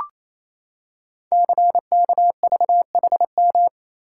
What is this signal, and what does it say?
Morse code, heard as a keyed tone, about 28 wpm, 700 Hz. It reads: CKVHM
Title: Code Groups - 5 Letters - 28wpm